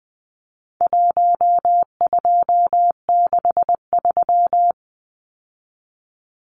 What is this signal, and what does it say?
Morse code, heard as a keyed tone, about 20 wpm, 700 Hz. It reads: 1263
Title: Numbers - 4-Digit 20wpm